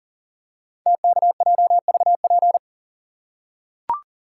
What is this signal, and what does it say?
Morse code, heard as a keyed tone, about 41 wpm, 700 Hz. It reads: TKJVP
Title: Code Groups - 5 Letters - 40wpm